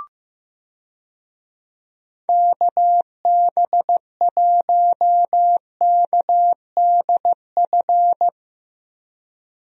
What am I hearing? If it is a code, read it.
Morse code, heard as a keyed tone, about 15 wpm, 700 Hz. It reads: KB1KDF
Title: USA Call Signs - 6 Characters - Repeated 3x - 15wpm